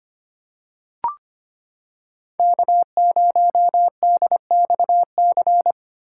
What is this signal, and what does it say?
Morse code, heard as a keyed tone, about 25 wpm, 700 Hz. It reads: K0DXC
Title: K1USN SST QSO Practice - 25wpm